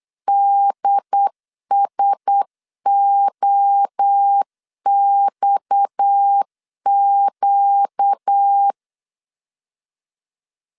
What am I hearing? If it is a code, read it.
Morse code, heard as a keyed tone, about 8 wpm, 800 Hz. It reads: DSOXQ
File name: CW / DET / 1 6 1500C S A 8wpm 5203